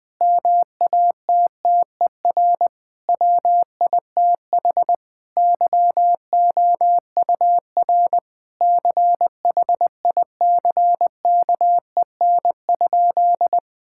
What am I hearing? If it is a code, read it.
Morse code, heard as a keyed tone, about 20 wpm, 700 Hz. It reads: MATTER WITH YOUR CHICKEN?
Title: Sentences from Top 1000 Words - Review Part 5 - 20wpm